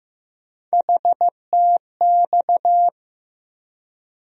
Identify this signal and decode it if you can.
Morse code, heard as a keyed tone, about 15 wpm, 700 Hz. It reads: HTX
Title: USA Call Sign Postfixes - 3 Letters 15wpm